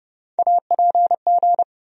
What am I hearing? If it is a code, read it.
Morse code, heard as a keyed tone, about 30 wpm, 700 Hz. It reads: APG
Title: USA Call Sign Postfixes - 3 Letters 30wpm